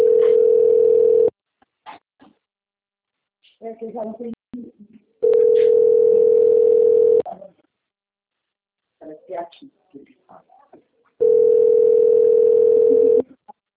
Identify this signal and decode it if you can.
Ringback tone (North America), 440 + 480 Hz, with speech.
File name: out- - - - - .1116